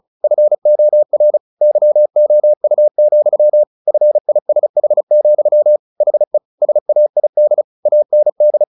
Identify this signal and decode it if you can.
Morse code, heard as a keyed tone, about 34 wpm, 600 Hz. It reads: FOR YOU, FISH, HE SAID AND
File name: the old man and the sea / the old man and the sea 1 35wpm